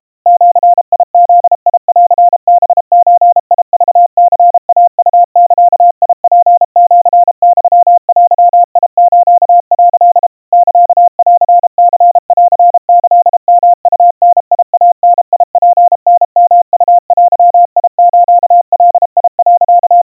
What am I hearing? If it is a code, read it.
Morse code (Japanese Wabun code), heard as a keyed tone, about 32 wpm, 700 Hz. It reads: ジブンハゾクニイウザツシユデス」 サンニンキヨウダイダツタヨウデスガ、
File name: kawa / Morse sample / kamyu 120 cpm 700 Hz Crispy